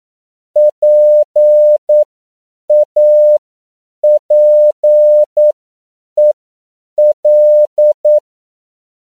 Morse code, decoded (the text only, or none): PAPEL